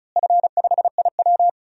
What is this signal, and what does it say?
Morse code, heard as a keyed tone, about 34 wpm, 700 Hz. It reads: F5IW
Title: DXCC-100 Call Signs - 4 Characters - Repeated 3x - 35wpm